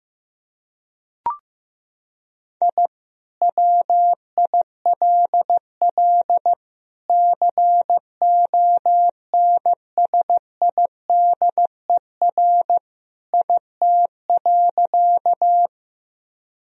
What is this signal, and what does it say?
Morse code, heard as a keyed tone, about 14 wpm, 700 Hz. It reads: I WILL CONSIDER IT.
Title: Sentences from Top 1000 Words - Review Part 4 - 15wpm